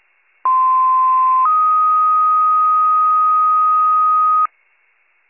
Two-tone page, heard as a one-second tone, then a three-second tone, about 1040 then 1250 Hz. It is heard from 0.4 to 4.5 s.